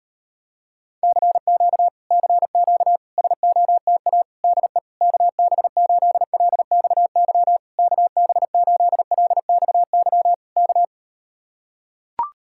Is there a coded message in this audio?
Yes: Morse code from 1.0 to 10.8 s, heard as a keyed tone, about 38 wpm, 700 Hz. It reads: CQ CQ SOTA DE KB8LXY KB8LXY K